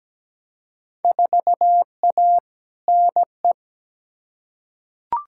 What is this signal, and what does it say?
Morse code, heard as a keyed tone, about 17 wpm, 700 Hz. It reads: 4A NE
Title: Field Day Exchanges - 17wpm